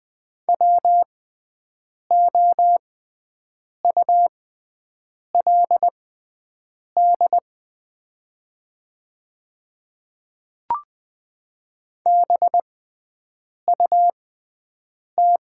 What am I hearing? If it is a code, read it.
Morse code, heard as a keyed tone, about 20 wpm, spaced as for 7 wpm, 690 Hz. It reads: WOULD BUT
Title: 6x Character Spacing - Top 100 Words - 20wpm